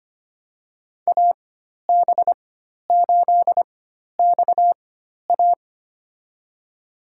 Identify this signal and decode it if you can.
Morse code, heard as a keyed tone, about 25 wpm, spaced as for 12 wpm, 700 Hz. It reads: AB8XA
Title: USA Call Signs - 5 Characters - 4x Character Spacing - 25wpm